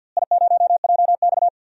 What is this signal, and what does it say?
Morse code, heard as a keyed tone, about 49 wpm, 700 Hz. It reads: I0JX
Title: DXCC-100 Call Signs - 4 Characters - Repeated 3x - 50wpm